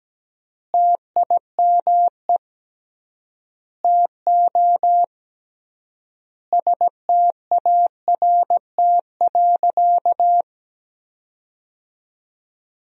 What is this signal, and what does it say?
Morse code, heard as a keyed tone, about 17 wpm, 700 Hz. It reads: TIME TO START.
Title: 3x Word-Spacing - Sentences from Top 300 Words - 17wpm